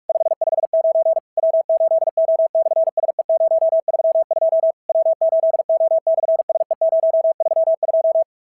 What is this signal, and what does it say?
Morse code, heard as a keyed tone, about 46 wpm, 650 Hz. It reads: 559 W8O/SE032 W8O/SE032
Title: SOTA QSO Practice 45wpm